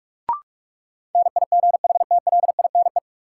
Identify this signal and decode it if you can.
Morse code, heard as a keyed tone, about 45 wpm, 700 Hz. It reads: NIGHTLINE
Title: Sets of 5 Words - New Word Drill - 45wpm